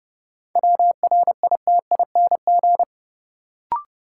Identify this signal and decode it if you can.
Morse code, heard as a keyed tone, about 31 wpm, 700 Hz. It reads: WRITING
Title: Sentences from Top 800 Words - New Word Drill - 30wpm